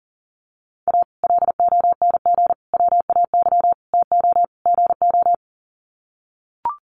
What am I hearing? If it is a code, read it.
Morse code, heard as a keyed tone, about 40 wpm, 700 Hz. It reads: A LONG WAY TO GO
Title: Sets of 5 Words 40wpm